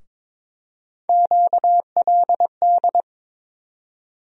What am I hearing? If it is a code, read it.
Morse code, heard as a keyed tone, about 22 wpm, 700 Hz. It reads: QLD
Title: USA Call Sign Postfixes - 3 Letters 22wpm